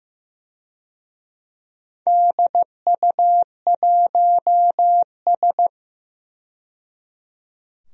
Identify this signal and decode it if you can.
Morse code, heard as a keyed tone, about 15 wpm, 700 Hz. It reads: DU1S
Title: DXCC-100 Call Signs - 4 Characters - 15wpm